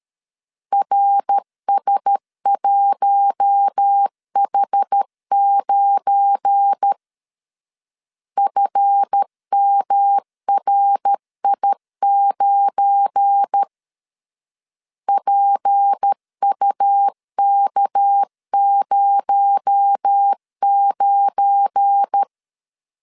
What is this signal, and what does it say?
Morse code, heard as a keyed tone, about 13 wpm, 800 Hz. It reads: RS1H9 FMRI9 PUK09
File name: LEZIONI / 7 GF9Z / 7 GF9 12wpm 2319